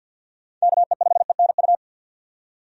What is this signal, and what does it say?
Morse code, heard as a keyed tone, about 53 wpm, 700 Hz. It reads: KE5ENU
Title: USA Call Signs - 6 Characters - 50wpm